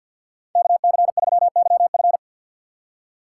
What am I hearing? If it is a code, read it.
Morse code, heard as a keyed tone, about 53 wpm, 700 Hz. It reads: KC3YF
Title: USA Call Signs - 5 Characters - 4x Character Spacing - 50wpm